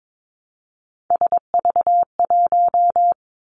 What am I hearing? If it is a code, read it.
Morse code, heard as a keyed tone, about 22 wpm, 700 Hz. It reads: SV1
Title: Callsign Prefixes for Top 50 DXCC - 22wpm